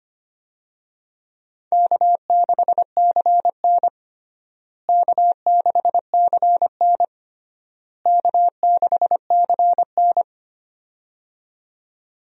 Morse code, heard as a keyed tone, about 25 wpm, 700 Hz. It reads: K6CN K6CN K6CN
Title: USA Call Signs - 4 Characters - Repeated 3x - 25wpm